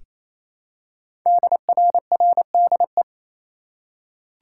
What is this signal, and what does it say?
Morse code, heard as a keyed tone, about 29 wpm, 700 Hz. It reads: DRRDE
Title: Code Groups - 5 Letters - 28wpm